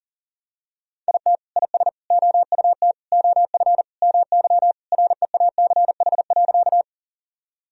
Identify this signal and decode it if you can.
Morse code, heard as a keyed tone, about 40 wpm, 700 Hz. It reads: IT IS OUT OF MY REACH.
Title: Sentences from Top 1000 Words - Review Part 2 - 40wpm